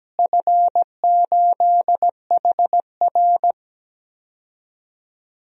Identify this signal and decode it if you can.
Morse code, heard as a keyed tone, about 17 wpm, 700 Hz. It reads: F8HR
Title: DXCC-100 Call Signs - 4 Characters - Repeated 3x - 17wpm